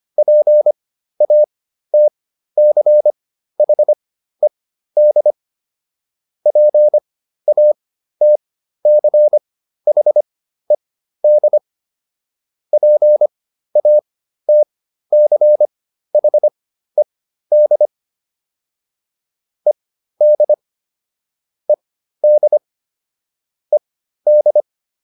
Morse code, heard as a keyed tone, about 25 wpm, spaced as for 13 wpm, 600 Hz. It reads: PATCHED PATCHED PATCHED ED ED ED